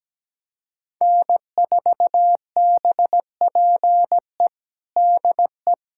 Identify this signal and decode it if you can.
Morse code, heard as a keyed tone, about 17 wpm, 700 Hz. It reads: N4BPE DE
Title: USA Callsign DE Callsign 17wpm